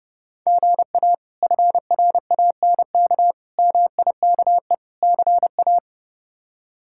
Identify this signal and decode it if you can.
Morse code, heard as a keyed tone, about 32 wpm, 700 Hz. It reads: GA FRANK MIKE CA
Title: K1USN SST QSO Practice - 30wpm